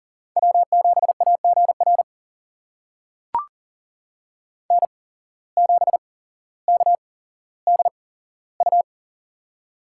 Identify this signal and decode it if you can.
Morse code, heard as a keyed tone, about 38 wpm, spaced as for 17 wpm, 700 Hz. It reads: W7AGR N7KDU
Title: USA Call Signs - 5 Characters - 8x Character Spacing - 40wpm